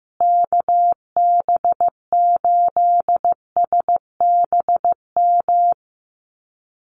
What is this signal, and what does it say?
Morse code, heard as a keyed tone, about 15 wpm, 700 Hz. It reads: KB8SBM
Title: USA Call Signs Repeated 2x 15wpm